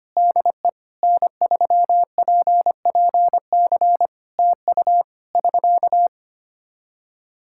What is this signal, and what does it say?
Morse code, heard as a keyed tone, about 25 wpm, 700 Hz. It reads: DE N3PPC TU <SK>
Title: K1USN SST QSO Practice 25wpm